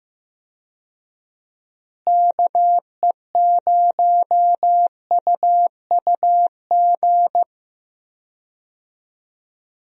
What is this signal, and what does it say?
Morse code, heard as a keyed tone, about 15 wpm, 700 Hz. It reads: KE0UUG
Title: USA Call Signs - 6 Characters - Repeated 2x - 15wpm